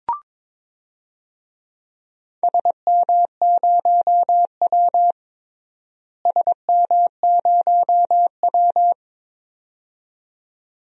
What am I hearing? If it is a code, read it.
Morse code, heard as a keyed tone, about 22 wpm, 700 Hz. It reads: SM0W SM0W
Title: DXCC-100 Call Signs - 4 Characters - Repeated 2x - 22wpm